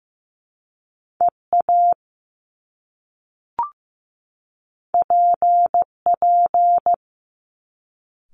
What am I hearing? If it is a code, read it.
Morse code, heard as a keyed tone, about 15 wpm, 700 Hz. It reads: EA PP
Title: Top Two Letter Combinations in Words 15wpm